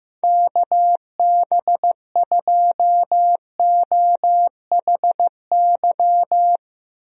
Morse code, heard as a keyed tone, about 15 wpm, 700 Hz. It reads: KB2OHY
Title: USA Call Signs - 6 Characters - Repeated 3x - 15wpm